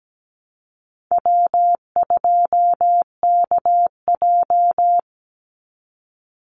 Morse code, heard as a keyed tone, about 17 wpm, 700 Hz. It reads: W2KJ
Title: Call Signs - 17wpm